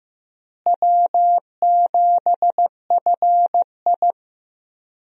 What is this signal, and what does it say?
Morse code, heard as a keyed tone, about 15 wpm, 700 Hz. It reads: W7FI
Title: USA Call Signs - 4 Characters - Repeated 3x - 15wpm